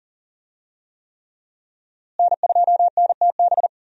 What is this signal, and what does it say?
Morse code, heard as a keyed tone, about 40 wpm, 700 Hz. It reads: N2NTB
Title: USA Call Signs - 5 Characters - Repeated 3x - 40wpm